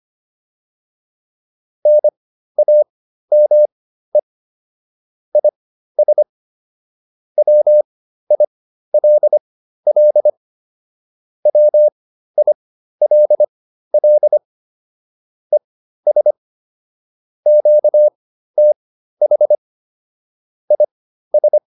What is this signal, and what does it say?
Morse code, heard as a keyed tone, about 25 wpm, spaced as for 13 wpm, 600 Hz. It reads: NAME IS WILL WILL ES QTH IS
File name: QSO409 13